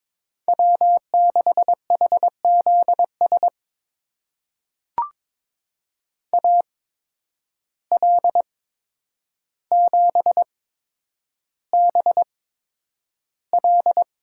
Morse code, heard as a keyed tone, about 22 wpm, spaced as for 9 wpm, 700 Hz. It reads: W6HZS AL7BL
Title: USA Call Signs - 5 Characters - 8x Character Spacing - 22wpm